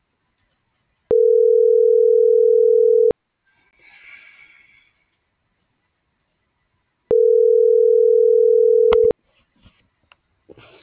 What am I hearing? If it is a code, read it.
Ringback tone (North America), 440 + 480 Hz.